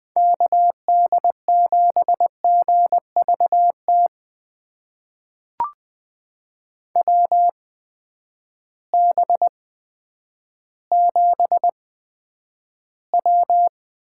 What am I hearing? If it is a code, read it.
Morse code, heard as a keyed tone, about 20 wpm, spaced as for 10 wpm, 700 Hz. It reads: KD7GVT WB7W
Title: USA Call Signs - 6 Characters - 8x Character Spacing - 20wpm